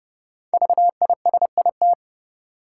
Morse code, heard as a keyed tone, about 30 wpm, 700 Hz. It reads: VISIT
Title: Sentences from Top 900 Words - New Word Drill - 30wpm